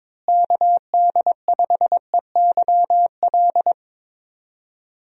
Morse code, heard as a keyed tone, about 22 wpm, 700 Hz. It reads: KD5EYL
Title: USA Call Signs - 6 Characters - Repeated 3x - 22wpm